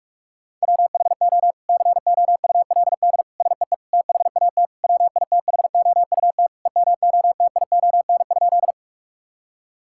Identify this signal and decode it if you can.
Morse code, heard as a keyed tone, about 41 wpm, 700 Hz. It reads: WHO COULD SEE THAT WITHOUT EMOTION?
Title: Sentences from Top 1600 Words 45wpm